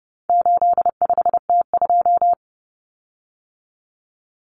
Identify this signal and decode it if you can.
Morse code, heard as a keyed tone, about 30 wpm, 700 Hz. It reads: 85T2
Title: Common Cut Numbers - 4-Digit 30wpm